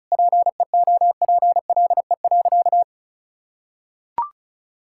Morse code, heard as a keyed tone, about 34 wpm, 700 Hz. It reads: PEOPLE.
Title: Sentences from Top 1000 Words - Review Part 6 - 35wpm